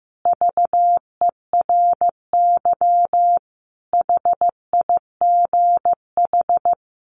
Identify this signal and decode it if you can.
Morse code, heard as a keyed tone, about 15 wpm, 700 Hz. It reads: VERY HIGH
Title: Sentences from Top 1300 Words 15wpm